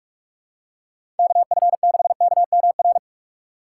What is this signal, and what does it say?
Morse code, heard as a keyed tone, about 46 wpm, 700 Hz. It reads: KF6KMR
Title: USA Call Signs - 6 Characters - 6x Character Spacing - 45wpm